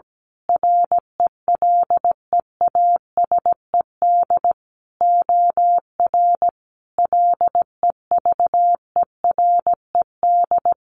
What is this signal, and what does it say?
Morse code, heard as a keyed tone, about 17 wpm, 700 Hz. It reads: RELEASED OR LEVERED
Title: A Sound Of Thunder 17wpm